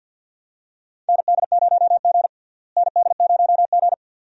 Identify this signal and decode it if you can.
Morse code, heard as a keyed tone, about 52 wpm, 700 Hz. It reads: ND0G ND0G
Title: USA Call Signs - 4 Characters - Repeated 2x - 50wpm